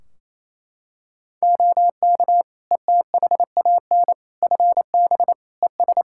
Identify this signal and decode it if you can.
Morse code, heard as a keyed tone, about 28 wpm, 700 Hz. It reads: OK ETHAN FB ES
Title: Practice QSO - Exchange 2 - Info - 28wpm